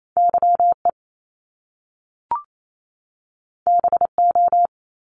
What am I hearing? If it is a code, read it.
Morse code, heard as a keyed tone, about 28 wpm, 700 Hz. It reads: YE BO
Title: Top Two Letter Combinations in Words 28wpm